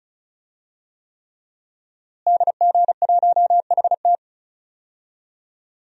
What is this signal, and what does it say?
Morse code, heard as a keyed tone, about 36 wpm, 700 Hz. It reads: DG1HT
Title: DXCC-100 Call Signs - 5 Characters - Repeated 3x - 35wpm